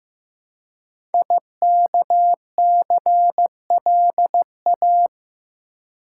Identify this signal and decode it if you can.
Morse code, heard as a keyed tone, about 15 wpm, 700 Hz. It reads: IKCLA
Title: Code Groups - 5 Letter:Numbers - 15wpm